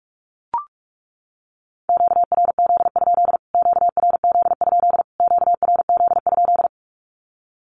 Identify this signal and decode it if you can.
Morse code, heard as a keyed tone, about 44 wpm, 700 Hz. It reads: QRZ? QRZ? QRZ?
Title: Q-Codes Repeated 3x 45wpm